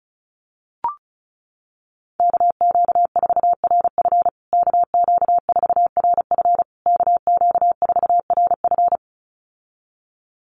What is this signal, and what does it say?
Morse code, heard as a keyed tone, about 34 wpm, 700 Hz. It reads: KQ4RF KQ4RF KQ4RF
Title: USA Call Signs Repeated 3x 35wpm